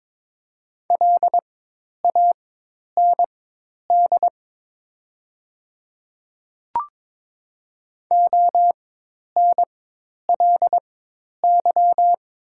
Morse code, heard as a keyed tone, about 22 wpm, spaced as for 10 wpm, 700 Hz. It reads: LAND ONLY
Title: 4x Character Spacing - Top 300 Words - 22wpm